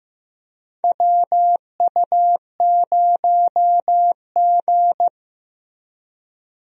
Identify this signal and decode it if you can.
Morse code, heard as a keyed tone, about 15 wpm, 700 Hz. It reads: WU0G
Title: USA Call Signs - 4 Characters - Repeated 2x - 15wpm